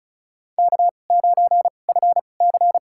Morse code, heard as a keyed tone, about 34 wpm, spaced as for 25 wpm, 700 Hz. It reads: K9FC
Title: USA Call Signs - 2x Character Spacing - 35wpm